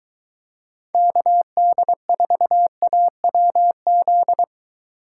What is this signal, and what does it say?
Morse code, heard as a keyed tone, about 23 wpm, 700 Hz. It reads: KD4AWZ